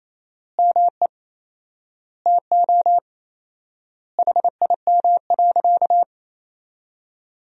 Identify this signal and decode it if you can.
Morse code, heard as a keyed tone, about 28 wpm, 700 Hz. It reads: ME TO HIM.
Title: 4x Word-Spacing - Sentences from Top 500 Words - 28wpm